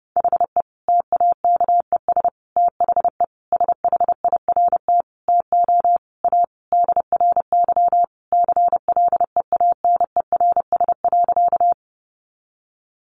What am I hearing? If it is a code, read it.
Morse code, heard as a keyed tone, about 30 wpm, 700 Hz. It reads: HE TAKES THE SHIRT TO A DRY CLEANERS.
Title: Additional Easy Phrases 30wpm